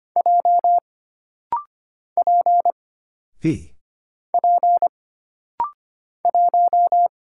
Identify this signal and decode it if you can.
Morse code, heard as a keyed tone, about 25 wpm, 700 Hz. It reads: J P P 1
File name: Single Character (Letter:Number:Punctuation:Prosign) - Rapid-Fire - 25wpm